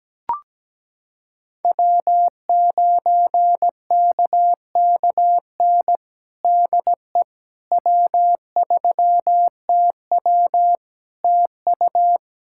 Morse code, heard as a keyed tone, about 17 wpm, 700 Hz. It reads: W9KKN DE W3TW TU
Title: Practice QSO - Exchange 4 - Ending - 17wpm